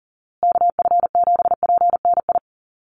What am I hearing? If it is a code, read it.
Morse code, heard as a keyed tone, about 40 wpm, 700 Hz. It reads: KF7PNI
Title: USA Call Signs Repeated 3x 40wpm